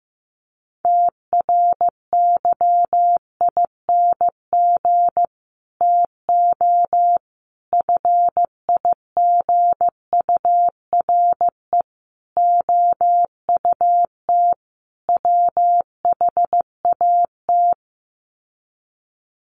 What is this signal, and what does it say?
Morse code, heard as a keyed tone, about 15 wpm, 700 Hz. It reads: TRYING TO FIGURE OUT WHAT
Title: Sets of 5 Words 15wpm